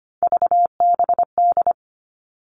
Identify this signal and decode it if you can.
Morse code, heard as a keyed tone, about 25 wpm, 700 Hz. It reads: VBD
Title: All Cut Numbers - 3-Digit 25wpm